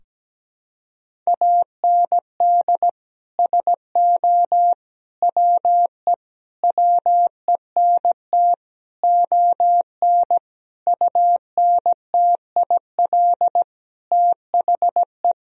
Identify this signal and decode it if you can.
Morse code, heard as a keyed tone, about 17 wpm, 700 Hz. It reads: AND SO WE WENT ON UNTIL THE
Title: Sentences from Top 1000 Words - Review Part 1 - 17wpm